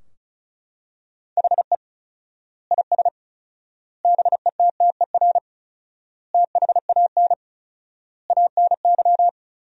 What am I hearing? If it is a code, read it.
Morse code, heard as a keyed tone, about 34 wpm, 700 Hz. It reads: HE IS BETTER THAN ANY
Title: 4x Word-Spacing - Sentences from Top 400 Words - 35wpm